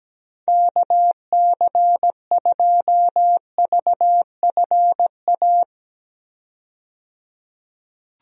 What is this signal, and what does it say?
Morse code, heard as a keyed tone, about 17 wpm, 700 Hz. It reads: KC2VFA
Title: USA Call Signs - 6 Characters - 17wpm